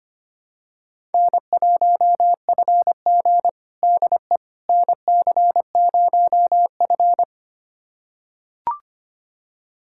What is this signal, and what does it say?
Morse code, heard as a keyed tone, about 25 wpm, 700 Hz. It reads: N1FG DE NC0F
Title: USA Callsign DE Callsign - 4 Characters - 25wpm